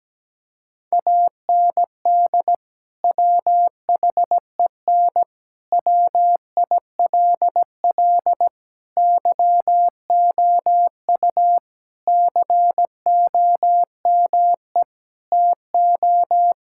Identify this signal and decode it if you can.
Morse code, heard as a keyed tone, about 17 wpm, 700 Hz. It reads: AND WHEN WILL YOU COME TO
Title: Sentences from Top 1600 Words 17wpm